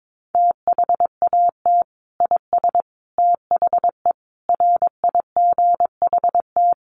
Morse code, heard as a keyed tone, about 22 wpm, 700 Hz. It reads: THAT IS THE RIGHT
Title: Sentences from Top 1400 Words 22wpm